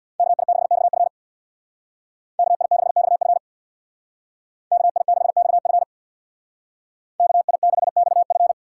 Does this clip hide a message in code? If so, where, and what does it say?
Yes: Morse code from 0.2 to 8.5 s, heard as a keyed tone, about 65 wpm, 700 Hz. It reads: KI6XF KI6XF KI6XF KI6XF